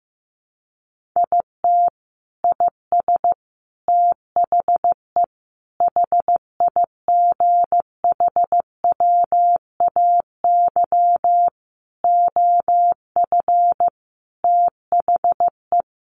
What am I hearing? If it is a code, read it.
Morse code, heard as a keyed tone, about 15 wpm, 700 Hz. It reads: IT IS THE HIGHWAY OF THE
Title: Sentences from Top 1400 Words 15wpm